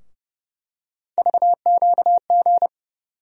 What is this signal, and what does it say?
Morse code, heard as a keyed tone, about 29 wpm, 700 Hz. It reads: VQG
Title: USA Call Sign Postfixes - 3 Letters 30wpm